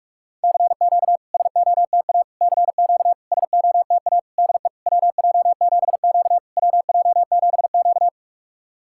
Morse code, heard as a keyed tone, about 44 wpm, 690 Hz. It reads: CQ SOTA CQ SOTA DE WJ7Q WJ7Q
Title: SOTA QSO Practice 45wpm